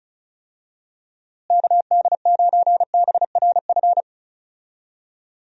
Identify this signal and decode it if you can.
Morse code, heard as a keyed tone, about 36 wpm, 700 Hz. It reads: KD9BRF
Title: USA Call Signs - 6 Characters - Repeated 2x - 35wpm